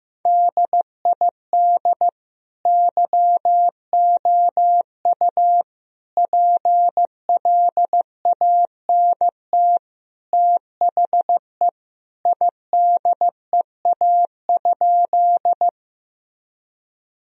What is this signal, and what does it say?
Morse code, heard as a keyed tone, about 15 wpm, 700 Hz. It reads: DID YOU PLANT THE IDEA?
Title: Sentences from Top 1000 Words - Review Part 1 - 15wpm